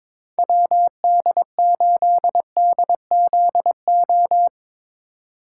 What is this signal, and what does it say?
Morse code, heard as a keyed tone, about 22 wpm, 700 Hz. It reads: WD8DZO
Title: USA Call Signs - 6 Characters - 3x Character Spacing - 22wpm